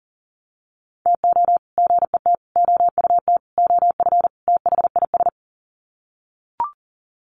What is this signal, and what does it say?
Morse code, heard as a keyed tone, about 40 wpm, 700 Hz. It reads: TO GET OUT OF THIS
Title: Sets of 5 Words Encore2 40wpm